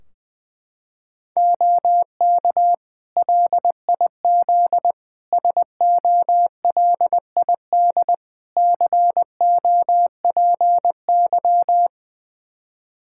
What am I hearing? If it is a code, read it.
Morse code, heard as a keyed tone, about 20 wpm, 700 Hz. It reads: OK LIZ SOLID COPY
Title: K1USN SST QSO Practice 20wpm